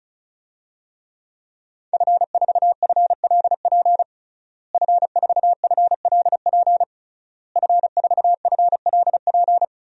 Morse code, heard as a keyed tone, about 34 wpm, 700 Hz. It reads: F4FLP F4FLP F4FLP
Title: DXCC-100 Call Signs - 5 Characters - Repeated 3x - 35wpm